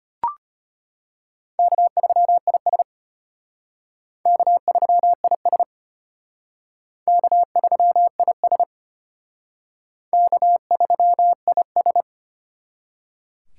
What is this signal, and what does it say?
Morse code, heard as a keyed tone, about 33 wpm, 700 Hz. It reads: K3IS K3IS K3IS K3IS